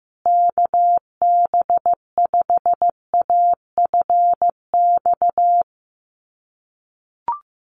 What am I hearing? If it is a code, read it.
Morse code, heard as a keyed tone, about 15 wpm, 700 Hz. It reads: KB5AFX
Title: USA Call Signs Repeated 3x 15wpm